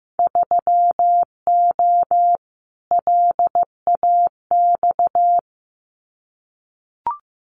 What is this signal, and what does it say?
Morse code, heard as a keyed tone, about 15 wpm, 700 Hz. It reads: 3O LAX
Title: Winter Field Day Exchanges Encore 15wpm